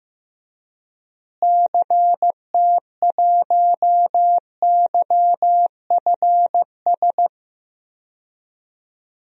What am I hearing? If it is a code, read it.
Morse code, heard as a keyed tone, about 15 wpm, 700 Hz. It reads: CT1YFS
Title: DXCC-100 Call Signs - 6 Characters - Repeated 3x - 15wpm